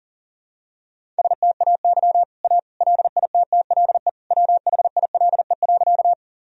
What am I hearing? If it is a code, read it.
Morse code, heard as a keyed tone, about 40 wpm, 700 Hz. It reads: STAY A LITTLE WHILE.
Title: Sentences from Top 1000 Words - Review Part 1 - 40wpm